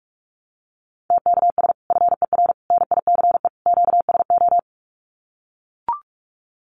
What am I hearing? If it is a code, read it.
Morse code, heard as a keyed tone, about 44 wpm, 700 Hz. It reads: TKS FER NICE QSO
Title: Common QSO Phrases 45wpm